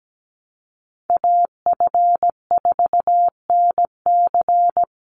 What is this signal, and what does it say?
Morse code, heard as a keyed tone, about 17 wpm, 700 Hz. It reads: AF4NC
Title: Call Signs - 17wpm